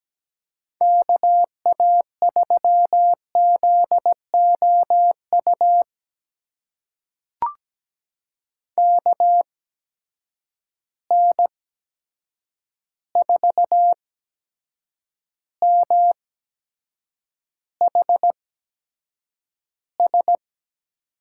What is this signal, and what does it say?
Morse code, heard as a keyed tone, about 17 wpm, spaced as for 7 wpm, 700 Hz. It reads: KA3ZOU KN4MHS
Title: USA Call Signs - 6 Characters - 8x Character Spacing - 17wpm